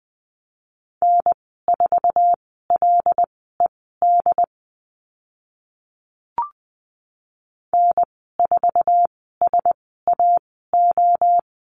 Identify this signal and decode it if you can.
Morse code, heard as a keyed tone, about 20 wpm, spaced as for 14 wpm, 700 Hz. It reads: N4LED N4SAO
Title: 076 Lesson 13g - Letter L - Call Signs 20wpm